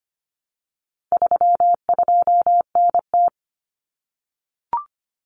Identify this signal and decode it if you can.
Morse code, heard as a keyed tone, about 25 wpm, 700 Hz. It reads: 32NT
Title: Common Cut Numbers - 4-Digit 25wpm